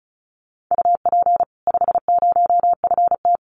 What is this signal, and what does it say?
Morse code, heard as a keyed tone, about 34 wpm, 700 Hz. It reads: UP 50FT
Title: Common QSO Phrases 35wpm